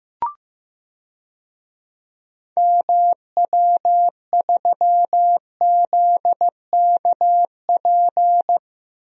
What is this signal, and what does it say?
Morse code, heard as a keyed tone, about 16 wpm, 700 Hz. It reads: MW3ZKP
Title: DXCC-100 Call Signs - 6 Characters - Repeated 2x - 15wpm